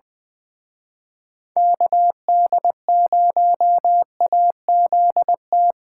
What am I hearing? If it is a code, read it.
Morse code, heard as a keyed tone, about 22 wpm, 700 Hz. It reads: KD0AZT
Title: Speed Racing - USA 2x3 Call Signs 20wpm